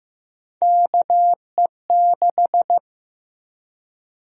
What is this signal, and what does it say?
Morse code, heard as a keyed tone, about 16 wpm, 700 Hz. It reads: KE6
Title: USA Call Sign Prefixes - 15wpm